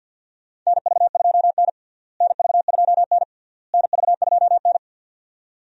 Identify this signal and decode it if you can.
Morse code, heard as a keyed tone, about 47 wpm, 700 Hz. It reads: NV2N NV2N NV2N
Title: USA Call Signs - 4 Characters - Repeated 3x - 50wpm